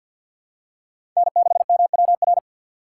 Morse code, heard as a keyed tone, about 51 wpm, 700 Hz. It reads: N6MWR
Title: USA Call Signs - 5 Characters - 2x Character Spacing - 50wpm